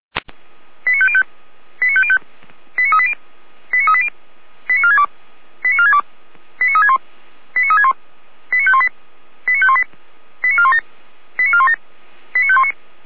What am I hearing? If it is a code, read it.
Five-tone selective call (ZVEI), heard as quick runs of five tones, tones of about 70 ms. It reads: 89505 89505 89380 89380 89462 89462 89361 89361 89518 89518 89417 89417 89519